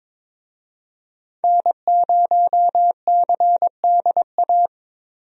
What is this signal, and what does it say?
Morse code, heard as a keyed tone, about 22 wpm, 700 Hz. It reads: N0CDA
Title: USA Call Signs - 5 Characters - Repeated 3x - 22wpm